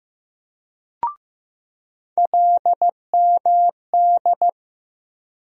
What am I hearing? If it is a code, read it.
Morse code, heard as a keyed tone, about 15 wpm, 700 Hz. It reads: LMD
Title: USA Call Sign Postfixes - 3 Letters 15wpm